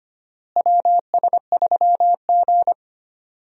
Morse code, heard as a keyed tone, about 25 wpm, 700 Hz. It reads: WS3G
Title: USA Call Signs - 4 Characters - Repeated 2x - 25wpm